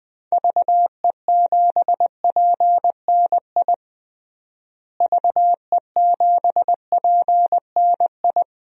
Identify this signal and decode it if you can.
Morse code, heard as a keyed tone, about 21 wpm, 700 Hz. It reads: VE7PNI VE7PNI
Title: DXCC-100 Call Signs - 6 Characters - Repeated 3x - 20wpm